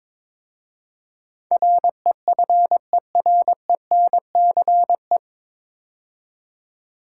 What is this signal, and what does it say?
Morse code, heard as a keyed tone, about 23 wpm, 700 Hz. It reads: REFERENCE
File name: 9 Letter Words - First 3 Letters Given - 22wpm